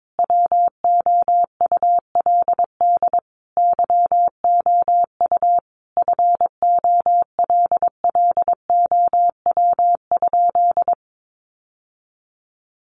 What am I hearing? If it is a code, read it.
Morse code, heard as a keyed tone, about 22 wpm, 700 Hz. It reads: WOULD YOU FOLLOW?
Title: Sentences from Top 1100 Words 22wpm